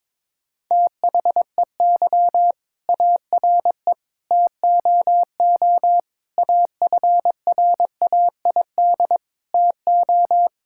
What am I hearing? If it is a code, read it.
Morse code, heard as a keyed tone, about 22 wpm, 700 Hz. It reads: THEY ARE TOO AFRAID TO
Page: morse-code-ninja-practice-22wpm